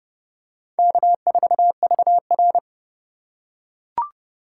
Morse code, heard as a keyed tone, about 29 wpm, 700 Hz. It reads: K4VR